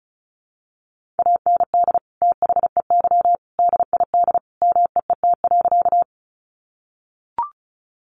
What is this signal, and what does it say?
Morse code, heard as a keyed tone, about 34 wpm, 700 Hz. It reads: AND THEY DID MEET.
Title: Sentences from Top 1100 Words 35wpm